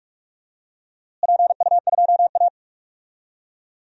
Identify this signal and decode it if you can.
Morse code, heard as a keyed tone, about 46 wpm, 700 Hz. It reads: PU2A
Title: DXCC-100 Call Signs - 4 Characters - Repeated 2x - 45wpm